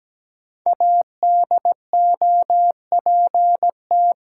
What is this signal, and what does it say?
Morse code, heard as a keyed tone, about 17 wpm, 700 Hz. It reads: ADOPT
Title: Five Letter Words - 17wpm